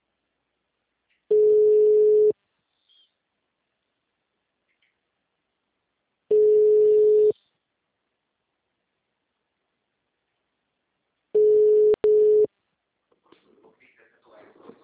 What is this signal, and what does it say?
Ringback tone (Europe, 425 Hz).